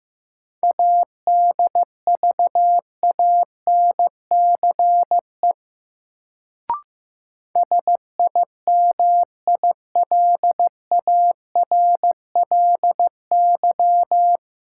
Morse code, heard as a keyed tone, about 15 wpm, 700 Hz. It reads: ADVANCE SIMILARLY
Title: Sentences from Top 1400 Words - New Word Drill - 15wpm